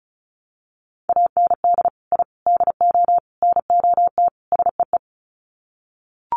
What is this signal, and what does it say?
Morse code, heard as a keyed tone, about 34 wpm, 700 Hz. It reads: AND I DO NOT SEE
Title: Sets of 5 Words Encore2 35wpm